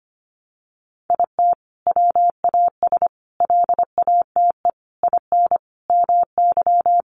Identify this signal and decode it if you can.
Morse code, heard as a keyed tone, about 25 wpm, 700 Hz. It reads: IT WAS LATE IN MY
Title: Sentences from Top 1300 Words 25wpm